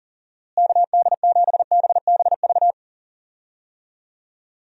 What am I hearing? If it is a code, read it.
Morse code, heard as a keyed tone, about 40 wpm, 700 Hz. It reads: KD7BBV
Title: USA Call Signs - 6 Characters - Repeated 2x - 40wpm